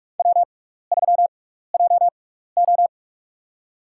Morse code, heard as a keyed tone, about 45 wpm, spaced as for 15 wpm, 700 Hz. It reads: W3JO
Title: USA Call Signs - 6x Character Spacing - 45wpm